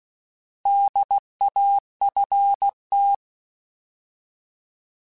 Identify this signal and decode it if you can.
Morse code, heard as a keyed tone, about 16 wpm, 800 Hz. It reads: DAFT
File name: Alphabetical / Alphabetical mp3s / Ds @15